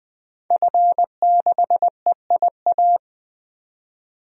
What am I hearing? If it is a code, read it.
Morse code, heard as a keyed tone, about 20 wpm, 700 Hz. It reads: F6EIA
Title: DXCC-100 Call Signs - 5 Characters - 20wpm